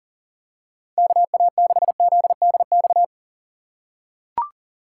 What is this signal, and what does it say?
Morse code, heard as a keyed tone, about 38 wpm, 700 Hz. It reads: KA6ZDX
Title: USA Call Signs - 6 Characters - 3x Character Spacing - 40wpm